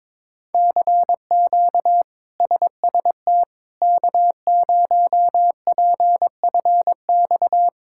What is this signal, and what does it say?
Morse code, heard as a keyed tone, about 22 wpm, 700 Hz. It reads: CQ SST K0PFX
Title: K1USN SST QSO Practice - 22wpm